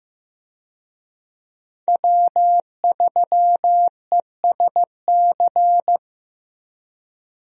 Morse code, heard as a keyed tone, about 16 wpm, 700 Hz. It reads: W3ESC
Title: USA Call Signs - 5 Characters - Repeated 2x - 15wpm